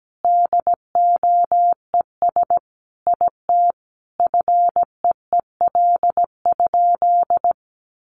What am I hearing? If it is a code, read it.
Morse code, heard as a keyed tone, about 17 wpm, 700 Hz. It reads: DOES IT FEEL?
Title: Sentences from Top 400 Words 17wpm